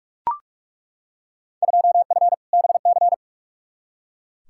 Morse code, heard as a keyed tone, about 41 wpm, 700 Hz. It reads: 2F BC
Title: Field Day Exchanges Encore - 45wpm